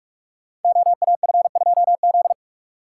Morse code, heard as a keyed tone, about 44 wpm, 700 Hz. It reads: OAF2Z
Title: Code Groups - 5 Letter:Numbers - 45wpm